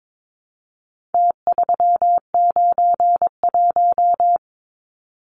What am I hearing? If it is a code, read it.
Morse code, heard as a keyed tone, about 22 wpm, 700 Hz. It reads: T391
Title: Numbers - Contest Style - 4-Digit 22wpm